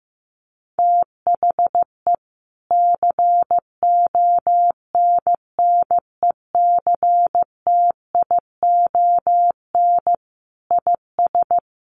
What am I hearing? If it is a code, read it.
Morse code, heard as a keyed tone, about 15 wpm, 700 Hz. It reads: THE CONNECTION IS
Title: Sentences from Top 1300 Words 15wpm